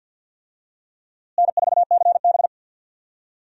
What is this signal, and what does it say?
Morse code, heard as a keyed tone, about 47 wpm, 700 Hz. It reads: N4CB
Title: USA Call Signs - 4 Characters - Repeated 3x - 50wpm